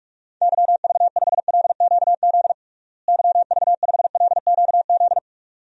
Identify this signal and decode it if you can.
Morse code, heard as a keyed tone, about 46 wpm, 700 Hz. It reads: YV5LQZ YV5LQZ
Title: DXCC-100 Call Signs - 6 Characters - Repeated 2x - 45wpm